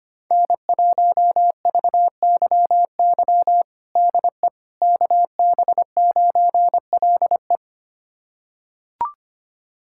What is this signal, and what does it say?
Morse code, heard as a keyed tone, about 25 wpm, 700 Hz. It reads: N1VYY DE KB9LE
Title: USA Callsign DE Callsign - 5 Characters - 25wpm